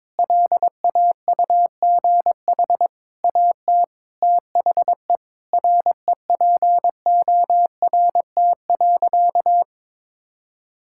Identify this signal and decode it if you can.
Morse code, heard as a keyed tone, about 22 wpm, 700 Hz. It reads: LAUGH AT THE REPORT.
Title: Sentences from Top 1000 Words - Review Part 6 - 22wpm